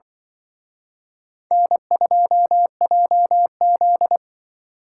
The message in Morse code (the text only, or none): N2JZ